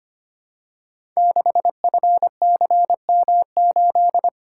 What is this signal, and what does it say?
Morse code, heard as a keyed tone, about 25 wpm, 700 Hz. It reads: 6FCM8
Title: Code Groups - 5 Letter:Numbers - 25wpm